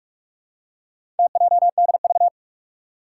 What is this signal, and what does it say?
Morse code, heard as a keyed tone, about 46 wpm, 700 Hz. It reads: TJDV
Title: Four Challenging Letter-Number Combinations 45wpm